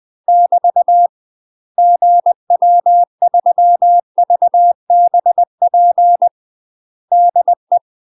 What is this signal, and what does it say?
Morse code, heard as a keyed tone, about 20 wpm, 700 Hz. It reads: <BT> GW3VBP DE
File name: UK MASTER QSOs TRAINING / 20WPM / UK MASTER QSOs 0060